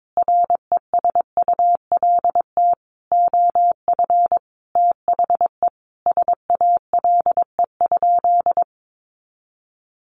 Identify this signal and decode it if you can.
Morse code, heard as a keyed tone, about 22 wpm, 700 Hz. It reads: RESULT OF THE SALE?
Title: Sentences from Top 1400 Words 22wpm